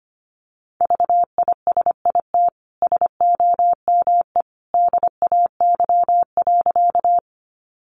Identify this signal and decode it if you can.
Morse code, heard as a keyed tone, about 25 wpm, 700 Hz. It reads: VISIT SOME DAY.
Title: Sentences from Top 900 Words 25wpm